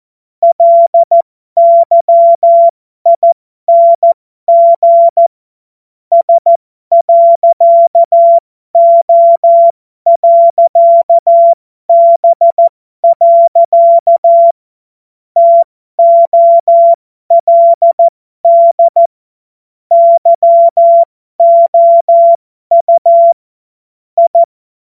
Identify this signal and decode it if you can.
Morse code, heard as a keyed tone, about 14 wpm, spaced as for 12 wpm, 680 Hz. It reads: LYING S.O.B. TOLD YOU I